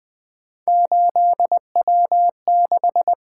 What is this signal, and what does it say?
Morse code, heard as a keyed tone, about 24 wpm, 700 Hz. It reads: 8W6
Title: Speed-Racing - Three Letter-Number Combinations - 20wpm